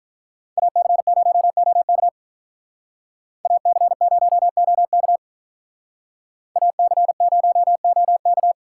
Morse code, heard as a keyed tone, about 44 wpm, 700 Hz. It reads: AC0OK AC0OK AC0OK